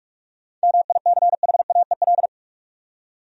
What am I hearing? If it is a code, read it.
Morse code, heard as a keyed tone, about 47 wpm, 700 Hz. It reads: MICHAEL
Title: USA Common Names - 45wpm